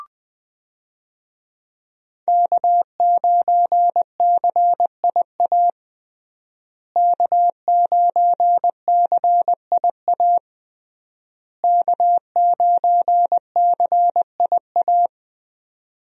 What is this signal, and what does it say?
Morse code, heard as a keyed tone, about 20 wpm, 700 Hz. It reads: K9CIA K9CIA K9CIA
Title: USA Call Signs - 5 Characters - Repeated 3x - 20wpm